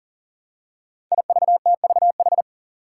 Morse code, heard as a keyed tone, about 40 wpm, 700 Hz. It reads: IVTVH
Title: Five Challenging Letter-Number Combinations 40wpm